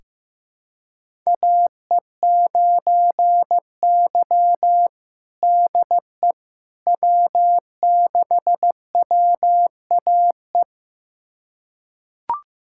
Morse code, heard as a keyed tone, about 15 wpm, 700 Hz. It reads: AE9Y DE W6WAE
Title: USA Callsign DE Callsign 15wpm